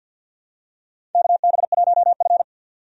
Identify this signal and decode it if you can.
Morse code, heard as a keyed tone, about 50 wpm, 700 Hz. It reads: KB1F
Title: USA Call Signs - 4 Characters - 50wpm